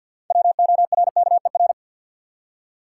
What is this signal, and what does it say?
Morse code, heard as a keyed tone, about 53 wpm, 700 Hz. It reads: WORKER
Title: Six Letter Words - 50wpm